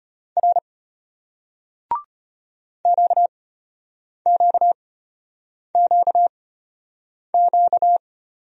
Morse code, heard as a keyed tone, about 33 wpm, spaced as for 8 wpm, 700 Hz. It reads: R QQQQ